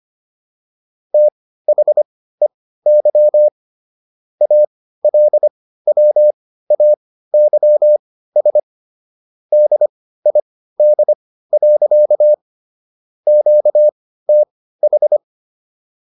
Morse code, heard as a keyed tone, about 25 wpm, spaced as for 15 wpm, 600 Hz. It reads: THEY ALWAYS DID. QTH